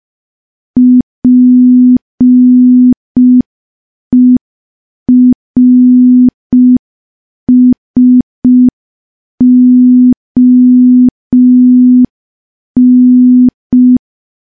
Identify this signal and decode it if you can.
Morse code, heard as a keyed tone, about 5 wpm, 250 Hz. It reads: PERSON
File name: morse / morse-training-05WPM